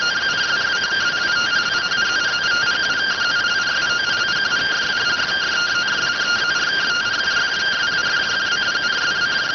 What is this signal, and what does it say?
Two-tone teleprinter signal, about 170 Hz shift.